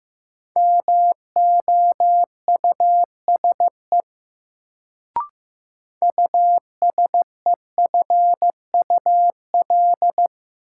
Morse code, heard as a keyed tone, about 15 wpm, 700 Hz. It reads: MOUSE USEFUL
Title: Sentences from Top 700 Words - New Word Drill - 15wpm